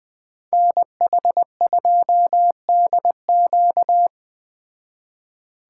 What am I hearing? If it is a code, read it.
Morse code, heard as a keyed tone, about 20 wpm, 700 Hz. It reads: NH2DQ